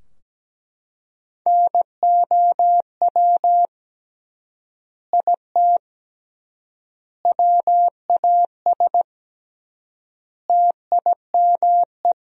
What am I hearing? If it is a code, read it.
Morse code, heard as a keyed tone, about 17 wpm, 700 Hz. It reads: NOW IT WAS TIME
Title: 3x Word-Spacing - Sentences from Top 200 Words - 17wpm